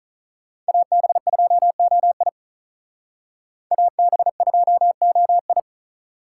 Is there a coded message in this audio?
Yes: Morse code from 0.7 to 5.6 s, heard as a keyed tone, about 44 wpm, 700 Hz. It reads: AB2OI AB2OI